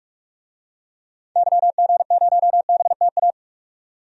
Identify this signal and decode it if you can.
Morse code, heard as a keyed tone, about 44 wpm, 690 Hz. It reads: YG0BTA
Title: DXCC-100 Call Signs - 6 Characters - 45wpm